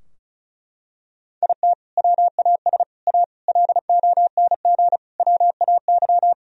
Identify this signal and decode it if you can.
Morse code, heard as a keyed tone, about 36 wpm, 700 Hz. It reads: IT WAS A LONG WAY
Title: Sentences from Top 1000 Words - Review Part 2 - 35wpm